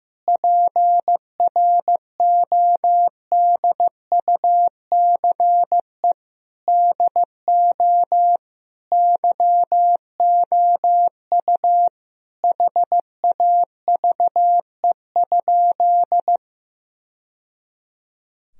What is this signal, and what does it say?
Morse code, heard as a keyed tone, about 16 wpm, 700 Hz. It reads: PRODUCE DO YOU HAVE?
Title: Sentences from Top 1000 Words - Review Part 3 - 15wpm